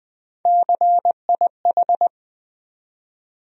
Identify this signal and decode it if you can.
Morse code, heard as a keyed tone, about 21 wpm, 700 Hz. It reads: CIH
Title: USA Call Sign Postfixes - 3 Letters 20wpm